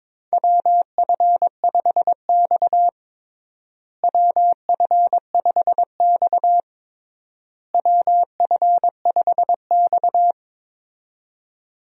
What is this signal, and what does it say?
Morse code, heard as a keyed tone, about 22 wpm, 700 Hz. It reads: WF5X WF5X WF5X
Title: USA Call Signs - 4 Characters - Repeated 3x - 22wpm